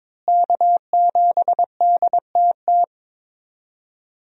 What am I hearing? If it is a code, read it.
Morse code, heard as a keyed tone, about 22 wpm, 700 Hz. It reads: K7DTT
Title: USA Call Signs - 5 Characters - Repeated 3x - 22wpm